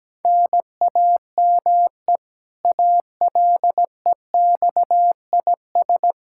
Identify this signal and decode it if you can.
Morse code, heard as a keyed tone, about 17 wpm, 700 Hz. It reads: NAME ALEXIS
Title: Practice QSO - Exchange 1 - Beginning - 17wpm